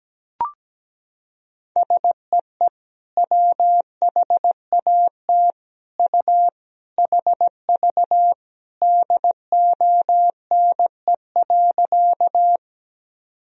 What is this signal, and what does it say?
Morse code, heard as a keyed tone, about 17 wpm, 700 Hz. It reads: SEE WHAT U HV DONE.
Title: Sentences from Top 500 Words and Abbreviations 17wpm